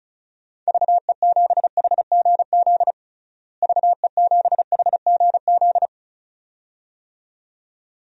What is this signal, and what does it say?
Morse code, heard as a keyed tone, about 36 wpm, 700 Hz. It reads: VE7HGZ VE7HGZ
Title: DXCC-100 Call Signs - 6 Characters - Repeated 2x - 35wpm